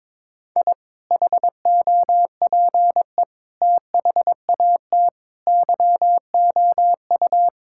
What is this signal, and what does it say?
Morse code, heard as a keyed tone, about 21 wpm, 700 Hz. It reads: I HOPE THAT YOU
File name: practice / 22wpm / Sentences with 2-Word Phrases - 22wpm